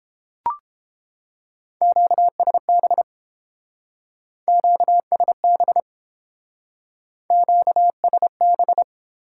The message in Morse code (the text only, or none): QSB QSB QSB